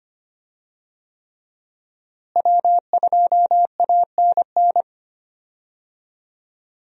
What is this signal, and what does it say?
Morse code, heard as a keyed tone, about 25 wpm, 700 Hz. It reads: W2ANN
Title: USA Call Signs - 5 Characters - Repeated 3x - 25wpm